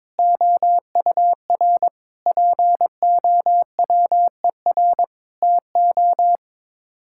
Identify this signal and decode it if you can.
Morse code, heard as a keyed tone, about 22 wpm, 700 Hz. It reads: OUR POWER TO
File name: practice / 22wpm / Sets of 8 Words - 22wpm